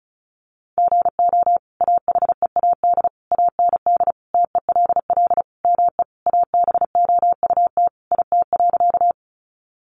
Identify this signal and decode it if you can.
Morse code, heard as a keyed tone, about 34 wpm, 700 Hz. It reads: GO AHEAD AND TELL ME ABOUT IT.
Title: Sentences from Top 1100 Words 35wpm